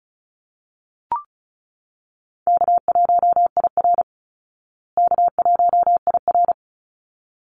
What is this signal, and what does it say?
Morse code, heard as a keyed tone, about 34 wpm, 700 Hz. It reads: K1IR K1IR
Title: Call Signs Repeated 3x - 35wpm